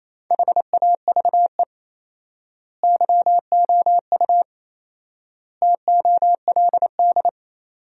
Morse code, heard as a keyed tone, about 28 wpm, 700 Hz. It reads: HAVE YOU TOLD
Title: 4x Word-Spacing - Sentences from Top 400 Words - 28wpm